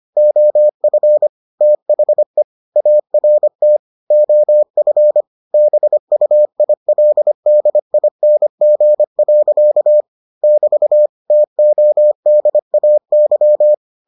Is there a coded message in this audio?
Yes: Morse code from 0.2 to 13.7 s, heard as a keyed tone, about 25 wpm, 600 Hz. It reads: OF THE ART OF BUILDING. <BT> TODAY